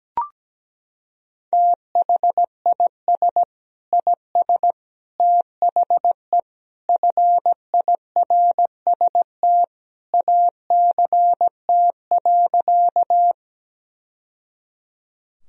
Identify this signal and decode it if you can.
Morse code, heard as a keyed tone, about 17 wpm, 700 Hz. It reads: THIS IS THE FIRST ACT.
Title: Sentences from Top 1000 Words - Review Part 1 - 17wpm